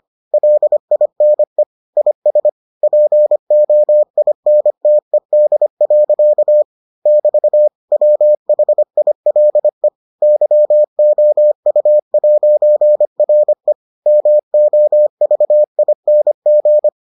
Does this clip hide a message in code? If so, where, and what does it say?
Yes: Morse code from 0.3 to 16.9 s, heard as a keyed tone, about 25 wpm, 600 Hz. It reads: LINE IS POINTED. <BT> WHILE YOU'RE MOVING